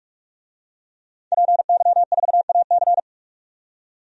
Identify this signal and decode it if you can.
Morse code, heard as a keyed tone, about 46 wpm, 700 Hz. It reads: PY4AC
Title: DXCC-100 Call Signs - 5 Characters - 45wpm